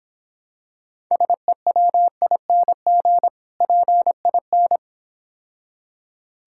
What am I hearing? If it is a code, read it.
Morse code, heard as a keyed tone, about 26 wpm, 700 Hz. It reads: SEWING PIN